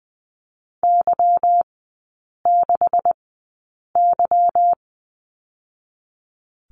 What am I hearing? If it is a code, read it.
Morse code, heard as a keyed tone, about 20 wpm, 700 Hz. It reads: Y 6 Y
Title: 145 Lesson 24e - Letter y - Three Characters 20wpm